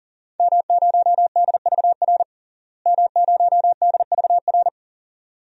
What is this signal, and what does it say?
Morse code, heard as a keyed tone, about 40 wpm, 700 Hz. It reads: M0DVR M0DVR
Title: DXCC-100 Call Signs - 5 Characters - Repeated 2x - 40wpm